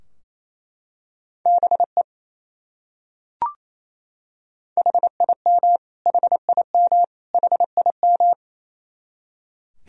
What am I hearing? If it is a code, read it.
Morse code, heard as a keyed tone, about 28 wpm, 700 Hz. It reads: BE HIM HIM HIM
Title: Top 100 Words Repeated 28wpm